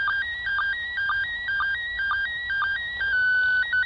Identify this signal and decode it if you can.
SSTV (slow-scan television), PD 120.
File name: ISS 2017-12-06 145 800MHz SSTV